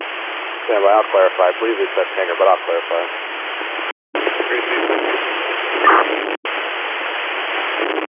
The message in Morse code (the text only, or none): none